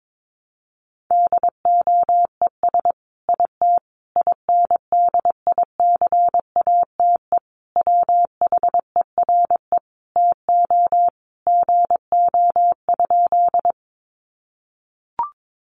Morse code, heard as a keyed tone, about 22 wpm, 700 Hz. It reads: DOES IT INDICATE WHERE TO GO?
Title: Sentences from Top 1300 Words 22wpm